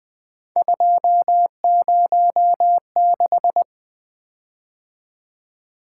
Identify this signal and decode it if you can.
Morse code, heard as a keyed tone, about 20 wpm, 700 Hz. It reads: 206
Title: Three Letter-Number Combinations - 20wpm